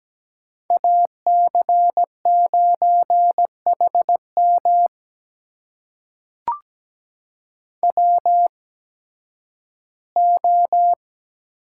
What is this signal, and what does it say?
Morse code, heard as a keyed tone, about 17 wpm, spaced as for 11 wpm, 700 Hz. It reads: AC9HM WO
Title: USA Call Signs - 5 Characters - 8x Character Spacing - 17wpm